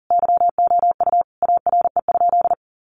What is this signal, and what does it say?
Morse code, heard as a keyed tone, about 40 wpm, 700 Hz. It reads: YOU ARE?
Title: Sentences from Top 700 Words 40wpm